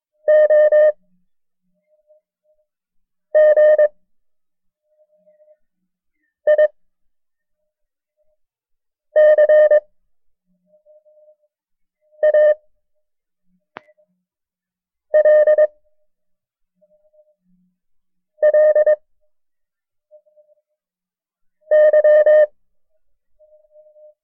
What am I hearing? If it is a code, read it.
Morse code, heard as a keyed tone, about 22 wpm, 610 Hz. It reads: O G I C A L L Y
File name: FreeFiles / CW-practice / KB6MT-tape1-side2